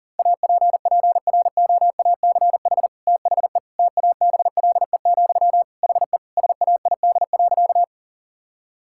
Morse code, heard as a keyed tone, about 37 wpm, 690 Hz. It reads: APPROACH THE TABLE, HE SAID.
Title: Sentences from Top 1000 Words - Review Part 6 - 40wpm